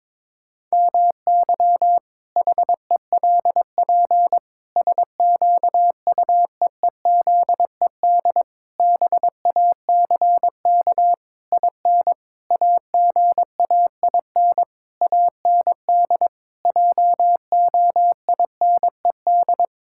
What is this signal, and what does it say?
Morse code, heard as a keyed tone, about 22 wpm, 700 Hz. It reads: MY HELP SQUEEZED BACK IN AGAIN AND JOINED
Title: Pangram Challenge v2 - 22wpm